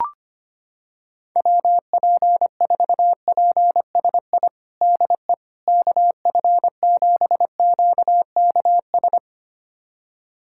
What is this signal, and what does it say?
Morse code, heard as a keyed tone, about 25 wpm, 700 Hz. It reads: WP4PSI DE KF7QKS
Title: USA Callsign DE Callsign - 6 Characters - 25wpm